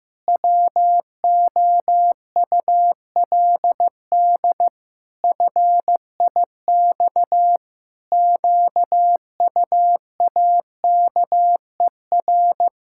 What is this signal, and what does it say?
Morse code, heard as a keyed tone, about 15 wpm, 700 Hz. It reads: WOULD FIX QUAKER
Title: Pangram Challenge v2 - 15wpm